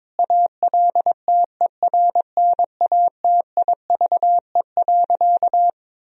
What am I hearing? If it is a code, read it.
Morse code, heard as a keyed tone, about 22 wpm, 700 Hz. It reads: ALTERNATIVE.
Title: Sentences from Top 1000 Words - Review Part 4 - 22wpm